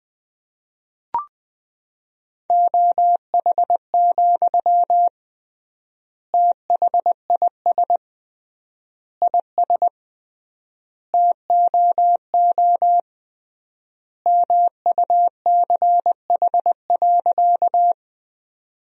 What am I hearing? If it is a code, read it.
Morse code, heard as a keyed tone, about 20 wpm, 700 Hz. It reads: OH, THIS IS TOO MUCH.
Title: 3x Word-Spacing - Sentences from Top 500 Words - 20wpm